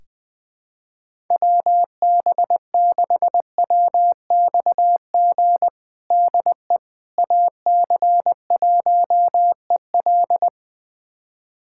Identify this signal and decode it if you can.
Morse code, heard as a keyed tone, about 21 wpm, 700 Hz. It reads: WB6WXG DE AC1EL
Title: USA Callsign DE Callsign 20wpm